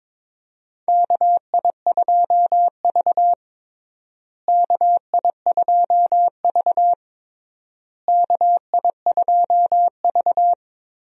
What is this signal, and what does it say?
Morse code, heard as a keyed tone, about 22 wpm, 700 Hz. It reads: KI2V KI2V KI2V
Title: USA Call Signs - 4 Characters - Repeated 3x - 22wpm